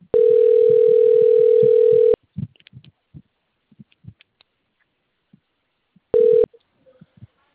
Ringback tone (North America), 440 + 480 Hz.